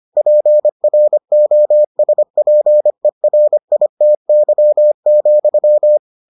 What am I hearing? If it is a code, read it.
Morse code, heard as a keyed tone, about 25 wpm, 600 Hz. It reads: PROSPERITY,